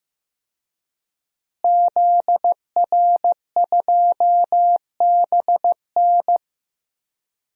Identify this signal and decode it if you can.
Morse code, heard as a keyed tone, about 14 wpm, 700 Hz. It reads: ZR2BN
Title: DXCC-100 Call Signs - 5 Characters - Repeated 3x - 15wpm